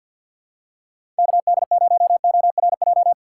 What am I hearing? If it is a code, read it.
Morse code, heard as a keyed tone, about 46 wpm, 700 Hz. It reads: KD0ORJ
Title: USA Call Signs - 6 Characters - 3x Character Spacing - 50wpm